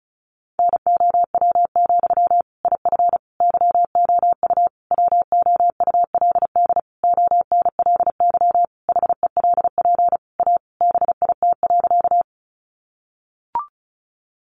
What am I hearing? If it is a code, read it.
Morse code, heard as a keyed tone, about 34 wpm, 700 Hz. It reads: NOW, IF YOU WOULD ONLY HELP A BIT.
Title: Sentences from Top 900 Words 35wpm